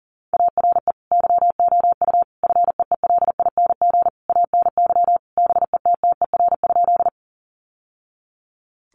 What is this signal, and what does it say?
Morse code, heard as a keyed tone, about 40 wpm, 700 Hz. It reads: ARE YOU FEELING ANY BETTER?
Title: Sentences from Top 1200 Words 40wpm